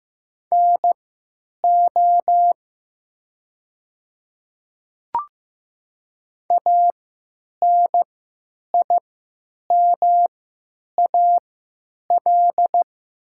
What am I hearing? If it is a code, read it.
Morse code, heard as a keyed tone, about 15 wpm, spaced as for 9 wpm, 700 Hz. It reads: NO ANIMAL
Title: 3x Character Spacing - Top 200 Words - 15wpm